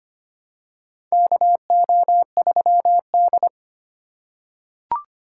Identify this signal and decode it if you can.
Morse code, heard as a keyed tone, about 25 wpm, 700 Hz. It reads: KO3D
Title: USA Call Signs - 4 Characters - 2x Character Spacing - 25wpm